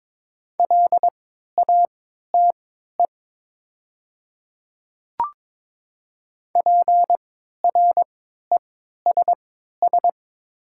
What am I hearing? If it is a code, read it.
Morse code, heard as a keyed tone, about 22 wpm, spaced as for 12 wpm, 700 Hz. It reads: LATE PRESS
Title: 3x Character Spacing - Top 300 Words - 22wpm